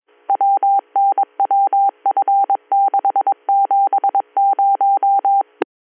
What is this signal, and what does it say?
Morse code, heard as a keyed tone, about 22 wpm, 790 Hz. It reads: WNWF670